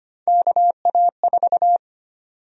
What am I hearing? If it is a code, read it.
Morse code, heard as a keyed tone, about 25 wpm, 700 Hz. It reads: KA4
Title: USA Call Sign Prefixes - Repeated 2x - 25wpm